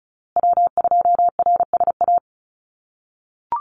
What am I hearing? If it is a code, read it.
Morse code, heard as a keyed tone, about 34 wpm, 700 Hz. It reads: W2RSA
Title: USA Call Signs Repeated 3x 35wpm